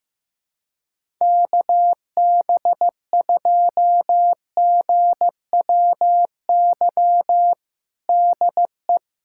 Morse code, heard as a keyed tone, about 15 wpm, 700 Hz. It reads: KB2GWY DE
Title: USA Callsign DE Callsign - 6 Characters - 15wpm